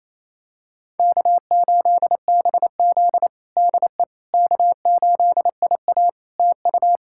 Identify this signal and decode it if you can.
Morse code, heard as a keyed tone, about 28 wpm, 700 Hz. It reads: K8BZ DE K8IA TU
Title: K1USN SST QSO Practice 28wpm